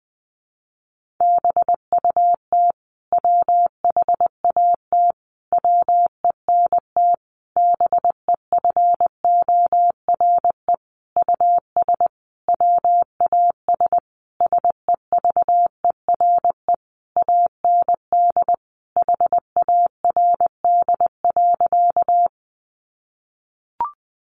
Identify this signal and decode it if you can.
Morse code, heard as a keyed tone, about 20 wpm, 700 Hz. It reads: BUT WHAT WENT BEFORE US WAS SEVERE AND HARD.
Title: Sentences from Top 1400 Words 20wpm